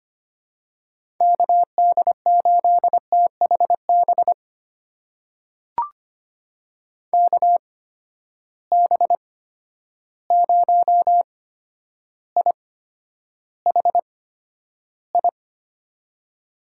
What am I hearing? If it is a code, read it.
Morse code, heard as a keyed tone, about 25 wpm, spaced as for 11 wpm, 700 Hz. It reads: KD8THB KB0IHI
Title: USA Call Signs - 6 Characters - 8x Character Spacing - 25wpm